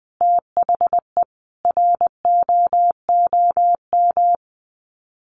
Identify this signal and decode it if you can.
Morse code, heard as a keyed tone, about 20 wpm, 700 Hz. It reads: THE ROOM
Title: Sets of 5 Words Encore 20wpm